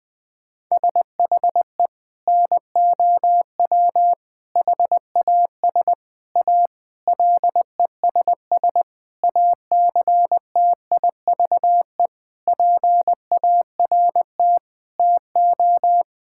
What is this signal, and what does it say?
Morse code, heard as a keyed tone, about 21 wpm, 700 Hz. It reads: SHE NOW HAS A LESS ACTIVE PART TO
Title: Sentences from Top 1000 Words - Review Part 6 - 20wpm